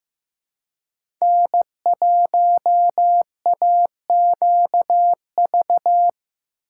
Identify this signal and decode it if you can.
Morse code, heard as a keyed tone, about 15 wpm, 700 Hz. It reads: N1AQV
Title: USA Call Signs - 2x Character Spacing - 15wpm